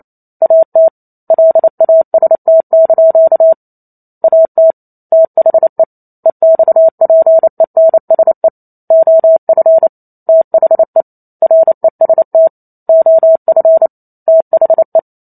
Morse code, heard as a keyed tone, about 28 wpm, 650 Hz. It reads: AT LAST) AT THE EXPENSE OF THE REST OF THE